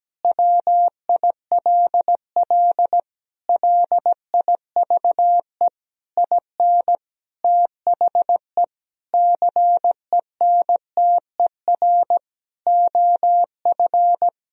Morse code, heard as a keyed tone, about 17 wpm, 700 Hz. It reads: WILL LIVE IN THE CENTER OF
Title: Sentences from Top 1000 Words - Review Part 2 - 17wpm